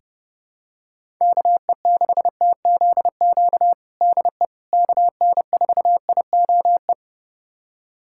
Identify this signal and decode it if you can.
Morse code, heard as a keyed tone, about 30 wpm, 700 Hz. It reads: KE6TZQ DE KN4IOE
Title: USA Callsign DE Callsign 30wpm